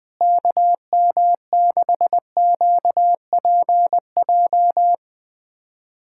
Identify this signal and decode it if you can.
Morse code, heard as a keyed tone, about 20 wpm, 700 Hz. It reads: KM6QPJ
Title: USA Callsign DE Callsign 20wpm